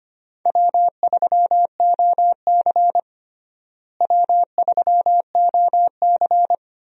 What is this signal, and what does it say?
Morse code, heard as a keyed tone, about 25 wpm, 700 Hz. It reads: W3OC W3OC
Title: USA Call Signs - 4 Characters - Repeated 2x - 25wpm